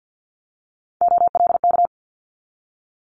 Morse code, heard as a keyed tone, about 50 wpm, 700 Hz. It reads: OLK
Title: Multi-letter Phonemes 50wpm